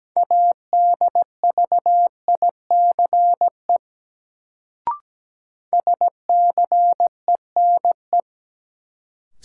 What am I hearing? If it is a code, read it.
Morse code, heard as a keyed tone, about 17 wpm, 700 Hz. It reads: ADVICE SCENE
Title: Sentences from Top 1100 Words - New Word Drill - 17wpm